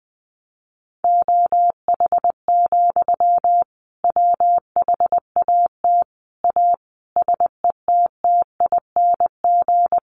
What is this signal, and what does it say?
Morse code, heard as a keyed tone, about 20 wpm, 700 Hz. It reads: OH, WHAT A SETTING
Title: Sentences from Top 1300 Words 20wpm